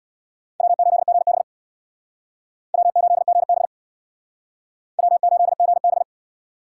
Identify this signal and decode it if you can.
Morse code, heard as a keyed tone, about 61 wpm, 700 Hz. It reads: W8GB W8GB W8GB
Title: Speed Racing - USA 1x2 Call Signs 50wpm